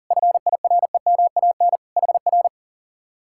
Morse code, heard as a keyed tone, about 50 wpm, 700 Hz. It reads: FIREMAN HR
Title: Speed-Racing Common QSO Phrases Encore - 40wpm